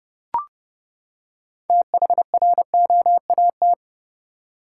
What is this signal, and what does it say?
Morse code, heard as a keyed tone, about 30 wpm, 700 Hz. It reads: THROAT
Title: Six Letter Words - 30wpm